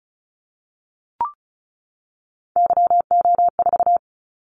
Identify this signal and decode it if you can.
Morse code, heard as a keyed tone, about 36 wpm, 700 Hz. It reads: YO4
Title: Callsign Prefixes for Top 25 DXCC - 35wpm